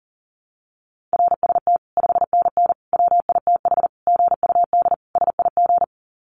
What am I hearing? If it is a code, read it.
Morse code, heard as a keyed tone, about 40 wpm, 700 Hz. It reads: RST 5NN WITH GUD SIG
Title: Common QSO Phrases 40wpm